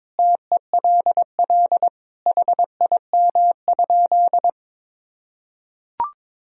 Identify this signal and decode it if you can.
Morse code, heard as a keyed tone, about 22 wpm, 700 Hz. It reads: TELL HIM?
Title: Sentences from Top 200 Words and Abbreviations 22wpm